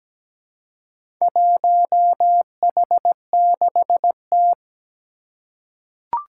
Morse code, heard as a keyed tone, about 17 wpm, 700 Hz. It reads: 1H6T
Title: Four Challenging Letter-Number Combinations 17wpm